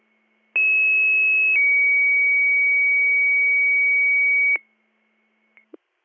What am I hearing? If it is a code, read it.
Two-tone page, heard as a one-second tone, then a three-second tone, about 2600 then 2250 Hz, with noise. It is heard from 0.6 to 4.6 s.